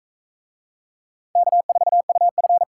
Morse code, heard as a keyed tone, about 36 wpm, 700 Hz. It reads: K4UF